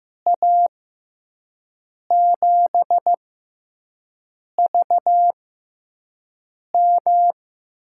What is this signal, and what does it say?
Morse code, heard as a keyed tone, about 15 wpm, spaced as for 5 wpm, 700 Hz. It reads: A7VM
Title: USA Call Signs - 6 Characters - 6x Character Spacing - 15wpm